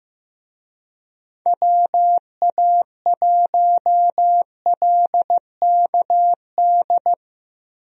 Morse code, heard as a keyed tone, about 15 wpm, 700 Hz. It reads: WA1LKD
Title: USA Call Signs - 6 Characters - Repeated 3x - 15wpm